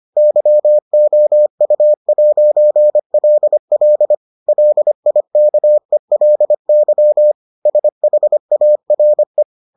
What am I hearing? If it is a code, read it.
Morse code, heard as a keyed tone, about 25 wpm, 600 Hz. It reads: YOU'LL LIKELY SHARE